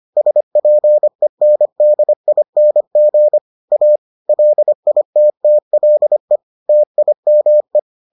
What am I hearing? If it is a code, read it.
Morse code, heard as a keyed tone, about 25 wpm, 600 Hz. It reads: SPENDING A LITTLE TIME